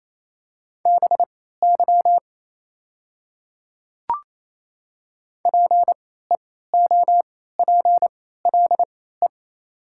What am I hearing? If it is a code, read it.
Morse code, heard as a keyed tone, about 27 wpm, spaced as for 16 wpm, 690 Hz. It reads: BY PEOPLE
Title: 3x Character Spacing - Top 200 Words - 28wpm